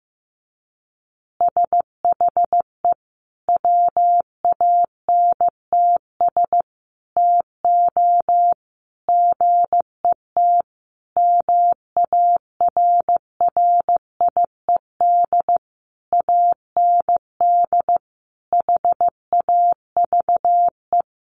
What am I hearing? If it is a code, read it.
Morse code, heard as a keyed tone, about 15 wpm, 700 Hz. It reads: SHE WANTS TO GET MARRIED AND HAVE
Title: Easy Phrases 15wpm